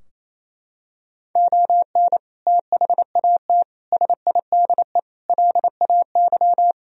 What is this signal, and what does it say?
Morse code, heard as a keyed tone, about 28 wpm, 700 Hz. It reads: ON THAT SIDE LAY
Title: Sentences from Top 1000 Words - Review Part 4 - 28wpm